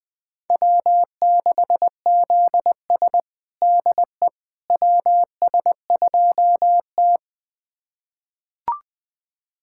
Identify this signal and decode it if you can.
Morse code, heard as a keyed tone, about 21 wpm, 700 Hz. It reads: W6ZS DE WS2T
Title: USA Callsign DE Callsign - 4 Characters - 20wpm